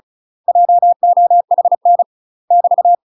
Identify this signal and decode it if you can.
Morse code, heard as a keyed tone, about 34 wpm, 700 Hz. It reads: JOHN <BT>
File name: UK MASTER QSOs TRAINING / 35WPM / UK MASTER QSOs 0009